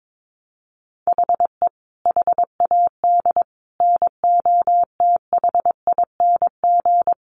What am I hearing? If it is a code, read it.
Morse code, heard as a keyed tone, about 22 wpm, 700 Hz. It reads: HE HAD NOTHING
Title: Sentences from Top 1100 Words 22wpm